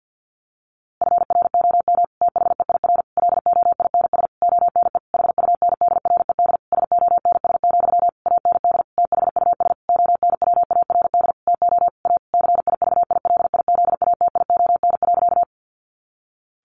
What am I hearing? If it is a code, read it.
Morse code, heard as a keyed tone, about 48 wpm, 700 Hz. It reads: FROM THEIR LOINS ONE HUNDRED SONS, AND THUS ONWARD TO A CIVILIZATION.
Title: A Sound Of Thunder 50wpm